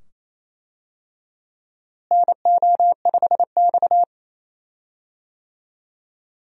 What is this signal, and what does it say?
Morse code, heard as a keyed tone, about 28 wpm, 700 Hz. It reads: NO5X
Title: USA Call Signs - 4 Characters - Repeated 2x - 28wpm